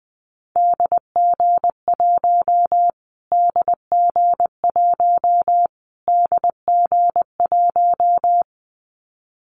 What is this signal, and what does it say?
Morse code, heard as a keyed tone, about 20 wpm, 700 Hz. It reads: DG1 DG1 DG1
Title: Callsign Prefixes for Top 25 DXCC - Repeated 3x - 20wpm